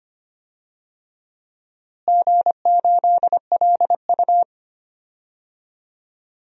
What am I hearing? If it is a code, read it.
Morse code, heard as a keyed tone, about 25 wpm, 700 Hz. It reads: G8LU
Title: DXCC-100 Call Signs - 4 Characters - Repeated 3x - 25wpm